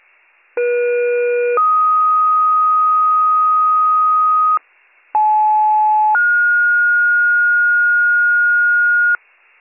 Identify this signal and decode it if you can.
Two-tone page, heard as a one-second tone, then a three-second tone, about 480 then 1250 Hz, about 860 then 1430 Hz, with noise. It is heard from 0.6 to 9.2 s.